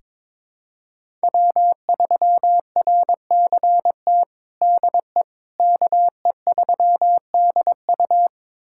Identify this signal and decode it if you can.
Morse code, heard as a keyed tone, about 22 wpm, 700 Hz. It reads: W3RCT DE KE3DU
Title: USA Callsign DE Callsign - 5 Characters - 22wpm